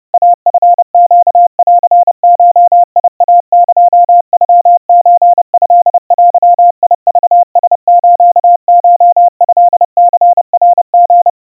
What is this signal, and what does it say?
Morse code (Japanese Wabun code), heard as a keyed tone, about 30 wpm, 700 Hz. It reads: イチネンゴイエノソトデクラスコトニナリ
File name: kawa / Morse sample / kamyu 110 cpm 700 Hz Crispy